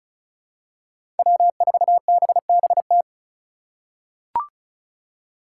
Morse code, heard as a keyed tone, about 36 wpm, 700 Hz. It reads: W4BBT
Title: K1USN SST QSO Practice - 35wpm